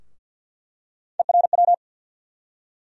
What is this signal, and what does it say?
Morse code, heard as a keyed tone, about 48 wpm, 690 Hz. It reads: ERW
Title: USA Call Sign Postfixes - 3 Letters 50wpm